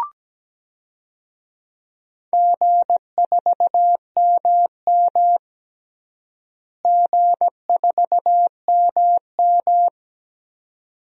Morse code, heard as a keyed tone, about 17 wpm, 700 Hz. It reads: G4MM G4MM
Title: DXCC-100 Call Signs - 4 Characters - Repeated 2x - 17wpm